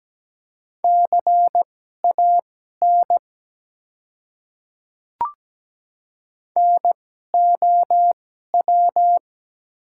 Morse code, heard as a keyed tone, about 17 wpm, spaced as for 12 wpm, 700 Hz. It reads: CAN NOW
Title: 2x Character Spacing - Top 100 Words - 17wpm